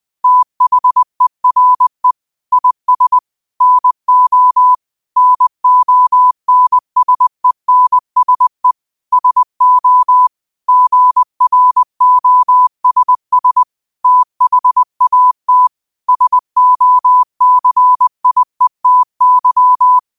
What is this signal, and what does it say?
Morse code, heard as a keyed tone, about 20 wpm, 1000 Hz. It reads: THERE IS NO NONSENSE SO GROSS THAT SOCIETY